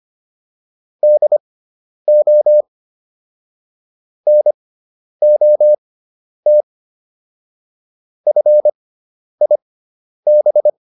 Morse code, heard as a keyed tone, about 25 wpm, spaced as for 10 wpm, 600 Hz. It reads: DO NOT FIB